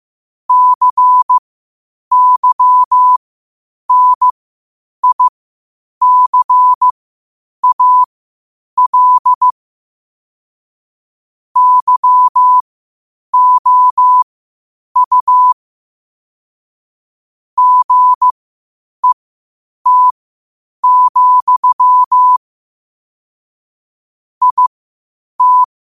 Morse code, heard as a keyed tone, about 15 wpm, spaced as for 8 wpm, 1000 Hz. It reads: CYNICAL YOU GET, IT
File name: qotdfiles / 8wpm